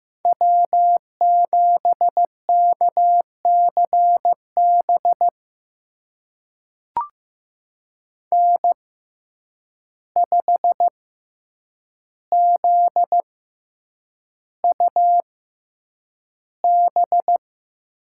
Morse code, heard as a keyed tone, about 15 wpm, spaced as for 8 wpm, 700 Hz. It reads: W7KCB N5ZUB
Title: USA Call Signs - 5 Characters - 6x Character Spacing - 15wpm